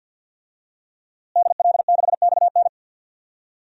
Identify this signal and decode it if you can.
Morse code, heard as a keyed tone, about 52 wpm, 700 Hz. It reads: DL6XN
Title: DXCC-100 Call Signs - 5 Characters - Repeated 2x - 50wpm